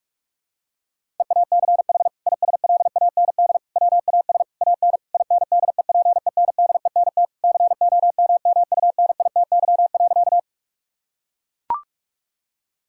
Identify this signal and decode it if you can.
Morse code, heard as a keyed tone, about 41 wpm, 700 Hz. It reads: EACH ISLAND WAS AN INDEPENDENT COMMUNITY.
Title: Sentences from Top 1000 Words - Review Part 6 - 45wpm